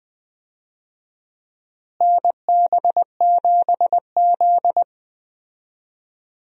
Morse code, heard as a keyed tone, about 20 wpm, 700 Hz. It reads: NB7Z
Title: USA Call Signs - 4 Characters - Repeated 2x - 20wpm